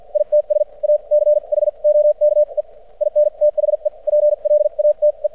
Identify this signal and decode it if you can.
Morse code, heard as a keyed tone, about 53 wpm, spaced as for 42 wpm, 600 Hz. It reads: ITS A CHOKE IN THE PLATE
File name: archived / CRSnet